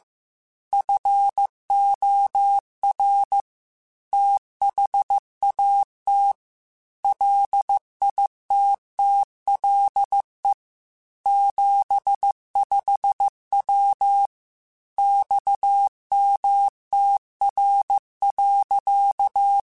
Morse code, heard as a keyed tone, about 15 wpm, 800 Hz. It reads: FOR THAT LITTLE 75W XMTR.
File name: QSOs / QSOs-mp3s / WA7QOX 1@13